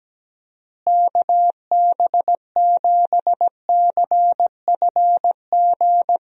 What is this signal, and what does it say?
Morse code, heard as a keyed tone, about 17 wpm, 700 Hz. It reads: KB7CFG
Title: USA Call Signs - 6 Characters - Repeated 3x - 17wpm